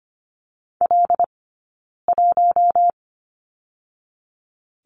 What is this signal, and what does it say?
Morse code, heard as a keyed tone, about 25 wpm, 700 Hz. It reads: L 1
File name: Lesson 15d - Number 5 - Two Characters 25wpm